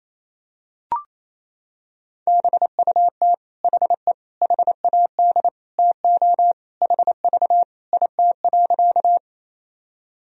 Morse code, heard as a keyed tone, about 27 wpm, 700 Hz. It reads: BUT HE HAD TO HV IT.
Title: Sentences from Top 100 Words and Abbreviations 28wpm